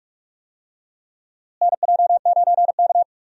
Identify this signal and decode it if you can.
Morse code, heard as a keyed tone, about 42 wpm, 700 Hz. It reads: NJ9K
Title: USA Call Signs - 4 Characters - Repeated 2x - 45wpm